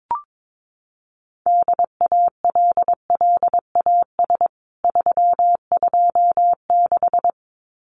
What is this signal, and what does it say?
Morse code, heard as a keyed tone, about 22 wpm, 700 Hz. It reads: DALLAS 326
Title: CWT Exchange 22wpm